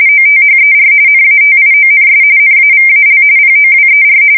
RTTY, about 45.45 baud. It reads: OTHER NATIONS NAVAL POWER
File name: fsk170 / usnavy5